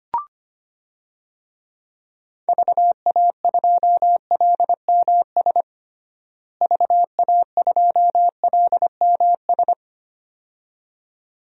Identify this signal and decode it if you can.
Morse code, heard as a keyed tone, about 25 wpm, 700 Hz. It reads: VA2LMS VA2LMS
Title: DXCC-100 Call Signs - 6 Characters - Repeated 2x - 25wpm